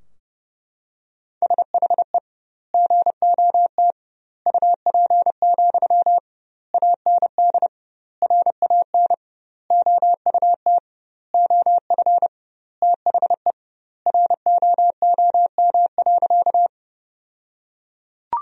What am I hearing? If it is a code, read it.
Morse code, heard as a keyed tone, about 31 wpm, 700 Hz. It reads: SHE GOT UP, AND RAN OUT OF THE ROOM.
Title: 2x Word-Spacing - Sentences from Top 500 Words - 30wpm